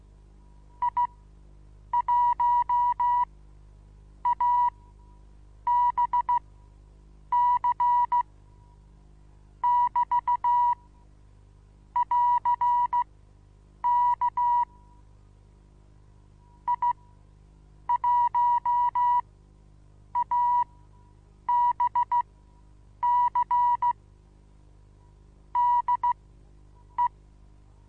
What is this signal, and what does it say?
Morse code, heard as a keyed tone, about 16 wpm, spaced as for 7 wpm, 970 Hz. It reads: I1ABC <BT> <AR>K I1ABC DE
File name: Onlinelehrgang / morsen / DJ4UF Morsen25